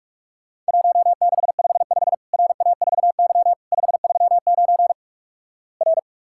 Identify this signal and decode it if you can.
Morse code, heard as a keyed tone, about 46 wpm, 700 Hz. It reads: 1655 RA4Y 539 R
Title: Worked All Europe DX Contest - QTC Exchange - 45wpm